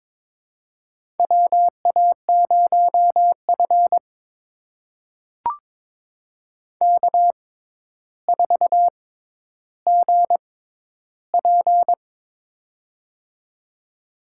Morse code, heard as a keyed tone, about 22 wpm, spaced as for 11 wpm, 700 Hz. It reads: WA0F K4GP
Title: USA Call Signs - 4 Characters - 6x Character Spacing - 22wpm